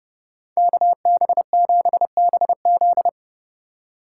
Morse code, heard as a keyed tone, about 30 wpm, 700 Hz. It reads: KB7BZ